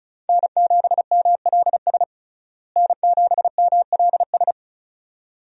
Morse code, heard as a keyed tone, about 34 wpm, 700 Hz. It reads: N7MLS N7MLS
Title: USA Call Signs - 5 Characters - Repeated 2x - 35wpm